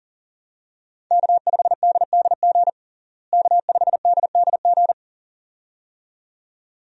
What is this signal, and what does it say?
Morse code, heard as a keyed tone, about 37 wpm, 700 Hz. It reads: K5DDG K5DDG
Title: USA Call Signs - 5 Characters - Repeated 2x - 40wpm